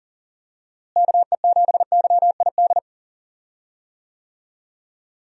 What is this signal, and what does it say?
Morse code, heard as a keyed tone, about 40 wpm, 700 Hz. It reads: KE7YID
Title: USA Call Signs - 6 Characters - Repeated 3x - 40wpm